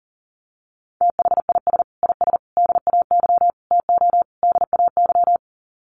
Morse code, heard as a keyed tone, about 40 wpm, 700 Hz. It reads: THIS IS DAY TO DAY
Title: Sets of 5 Words Encore 40wpm